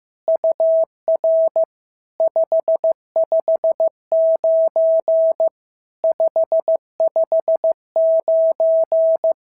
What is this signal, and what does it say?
Morse code, heard as a keyed tone, about 15 wpm, 650 Hz. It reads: UR 559 559
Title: SOTA QSO Practice 15wpm